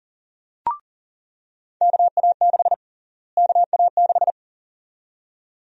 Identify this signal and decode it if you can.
Morse code, heard as a keyed tone, about 40 wpm, 700 Hz. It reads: KA6 KA6
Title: USA Call Sign Prefixes - Repeated 2x - 40wpm